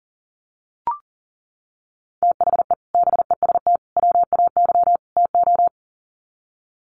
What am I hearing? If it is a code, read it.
Morse code, heard as a keyed tone, about 40 wpm, 700 Hz. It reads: THE BEST WAY TO
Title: Sets of 4 Words 40wpm